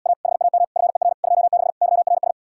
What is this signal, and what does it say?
Morse code, heard as a keyed tone, about 77 wpm, 700 Hz. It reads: A HAM FER 26 YRS